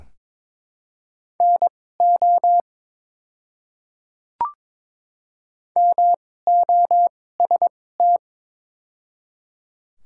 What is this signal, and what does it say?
Morse code, heard as a keyed tone, about 22 wpm, spaced as for 16 wpm, 690 Hz. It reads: NO MOST
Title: 2x Character Spacing - Top 100 Words - 22wpm